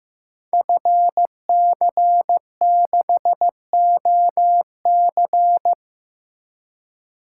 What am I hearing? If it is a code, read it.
Morse code, heard as a keyed tone, about 15 wpm, 700 Hz. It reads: FC6OC
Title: Code Groups - 5 Letter:Numbers - 15wpm